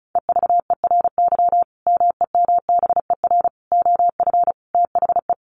Morse code, heard as a keyed tone, about 34 wpm, 700 Hz. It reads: EVERY MEMBER OF THE
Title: Sentences from Top 1400 Words 35wpm